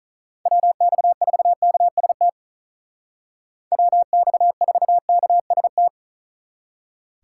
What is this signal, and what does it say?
Morse code, heard as a keyed tone, about 44 wpm, 700 Hz. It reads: WX4KST WX4KST
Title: Speed Racing - USA Call Signs 35wpm